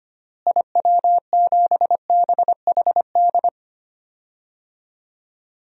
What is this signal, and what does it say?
Morse code, heard as a keyed tone, about 25 wpm, 700 Hz. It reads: IW7BHD
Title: DXCC-100 Call Signs - 6 Characters - Repeated 3x - 25wpm